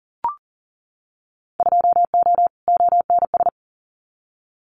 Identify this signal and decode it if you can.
Morse code, heard as a keyed tone, about 41 wpm, 700 Hz. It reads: 2O ONS
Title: Winter Field Day Exchanges Encore 40wpm